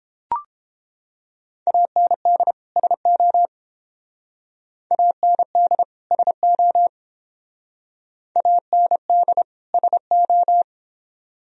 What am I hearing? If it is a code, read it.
Morse code, heard as a keyed tone, about 30 wpm, 700 Hz. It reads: AND SO AND SO AND SO